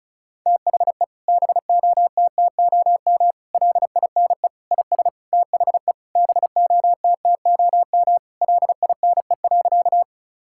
Morse code, heard as a keyed tone, about 36 wpm, 700 Hz. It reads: THE BOTTOM LINE IS THE BOTTOM LINE.
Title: Sentences from Top 1000 Words - Review Part 6 - 35wpm